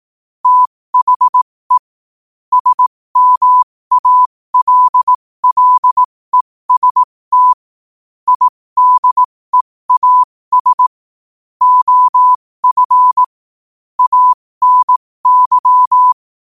Morse code, heard as a keyed tone, about 18 wpm, spaced as for 15 wpm, 1000 Hz. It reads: THE SMALLEST IDEAS OF ANY
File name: qotdfiles / 15wpm